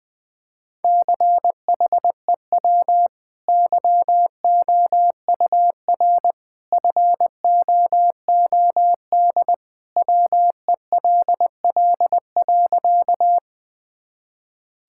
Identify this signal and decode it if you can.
Morse code, heard as a keyed tone, about 20 wpm, 700 Hz. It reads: CHEW YOUR FOOD WELL.
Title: Easy Sentences with Hints - 20wpm